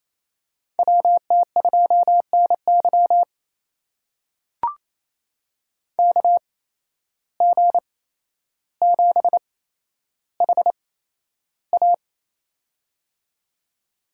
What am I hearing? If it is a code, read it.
Morse code, heard as a keyed tone, about 28 wpm, spaced as for 12 wpm, 700 Hz. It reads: WT2NY KG7HA
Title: USA Call Signs - 5 Characters - 8x Character Spacing - 28wpm